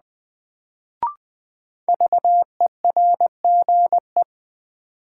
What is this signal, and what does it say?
Morse code, heard as a keyed tone, about 20 wpm, 700 Hz. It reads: VERGE
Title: Sets of 4 Words - New Word Drill - 20wpm